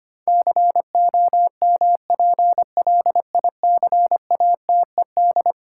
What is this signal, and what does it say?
Morse code, heard as a keyed tone, about 25 wpm, 700 Hz. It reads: COMPLICATED
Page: morse-code-ninja-practice-25wpm